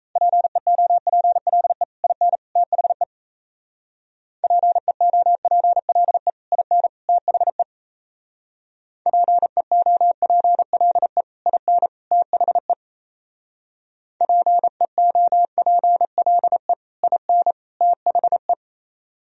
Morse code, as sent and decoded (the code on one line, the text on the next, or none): .--. --- .--. .-.. / .. -. / - .... / .--. --- .--. .-.. / .. -. / - .... / .--. --- .--. .-.. / .. -. / - .... / .--. --- .--. .-.. / .. -. / - .... .
PEOPLE IN THE PEOPLE IN THE PEOPLE IN THE PEOPLE IN THE